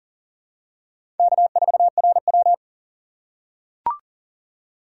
Morse code, heard as a keyed tone, about 40 wpm, 700 Hz. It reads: K4RW
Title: USA Call Signs - 4 Characters - 6x Character Spacing - 40wpm